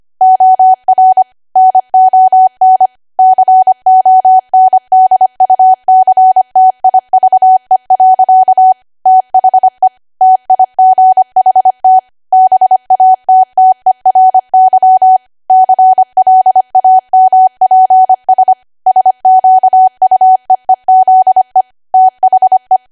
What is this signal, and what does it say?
Morse code, heard as a keyed tone, about 25 wpm, 750 Hz. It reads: OR NON CONDUCTIVE. THE TIGHT BATTERY CLAMPS SQUEEZE THE
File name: Archive / 25 WPM / 25WPM